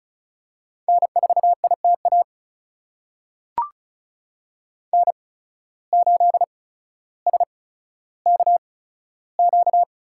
Morse code, heard as a keyed tone, about 36 wpm, spaced as for 15 wpm, 700 Hz. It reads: N4ITA N8SKQ
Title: USA Call Signs - 5 Characters - 8x Character Spacing - 35wpm